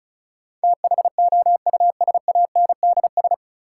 Morse code, heard as a keyed tone, about 36 wpm, 700 Hz. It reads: THOUSANDS
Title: Sets of 4 Words - New Word Drill - 35wpm